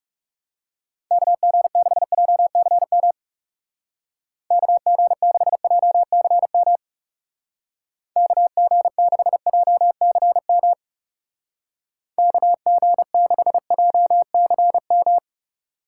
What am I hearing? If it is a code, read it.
Morse code, heard as a keyed tone, about 37 wpm, 700 Hz. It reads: KG6JCM KG6JCM KG6JCM KG6JCM
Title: Speed Racing - USA 2x3 Call Signs 30wpm